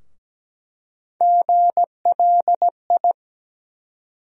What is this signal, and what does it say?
Morse code, heard as a keyed tone, about 17 wpm, 700 Hz. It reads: GLI
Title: USA Call Sign Postfixes - 3 Letters 17wpm